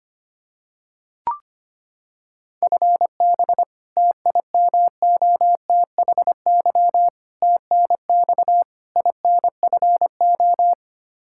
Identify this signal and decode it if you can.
Morse code, heard as a keyed tone, about 25 wpm, 700 Hz. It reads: FB TIMOTHY TNX INFO
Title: Practice QSO - Exchange 3 - More Info - 25wpm